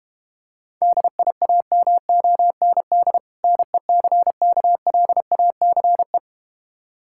Morse code, heard as a keyed tone, about 32 wpm, 700 Hz. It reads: DIAMOND NECKLACE